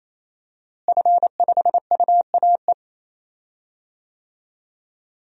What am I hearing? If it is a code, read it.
Morse code, heard as a keyed tone, about 28 wpm, 700 Hz. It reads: F5UAE
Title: DXCC-100 Call Signs - 5 Characters - Repeated 3x - 28wpm